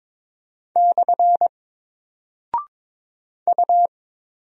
Morse code, heard as a keyed tone, about 22 wpm, 700 Hz. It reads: / U
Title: Single Character (Letter:Number:Punctuation:Prosign) 22wpm